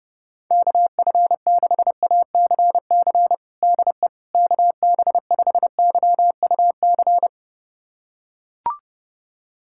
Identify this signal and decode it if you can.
Morse code, heard as a keyed tone, about 31 wpm, 700 Hz. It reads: KF6ACC DE KB5YUC
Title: USA Callsign DE Callsign 30wpm